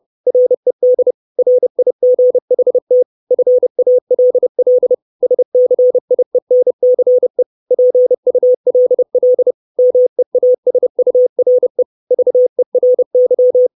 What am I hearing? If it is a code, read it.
Morse code, heard as a keyed tone, about 30 wpm, 500 Hz. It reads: RED RIGHT FALL SCIENCE PULL MEASURE VERY
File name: 632 Words 30WPM